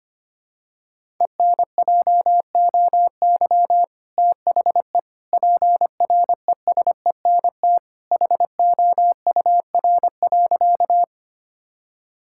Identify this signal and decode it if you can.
Morse code, heard as a keyed tone, about 25 wpm, 700 Hz. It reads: ENJOY THE PRESENT HOUR.
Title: Sentences from Top 1000 Words - Review Part 5 - 25wpm